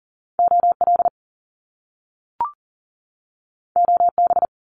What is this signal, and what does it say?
Morse code, heard as a keyed tone, about 40 wpm, 700 Hz. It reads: OL OB
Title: Top Two Letter Combinations in Words 40wpm